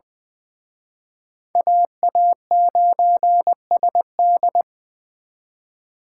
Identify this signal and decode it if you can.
Morse code, heard as a keyed tone, about 24 wpm, 700 Hz. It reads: AA9SD
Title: Speed Racing - USA 2x2 Call Signs 20wpm